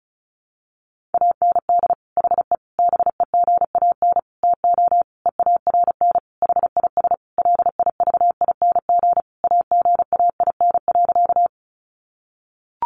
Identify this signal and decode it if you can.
Morse code, heard as a keyed tone, about 34 wpm, 700 Hz. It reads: AND HE BEGAN TO EARN HIS LIVING AGAIN.
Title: Sentences from Top 1500 Words 35wpm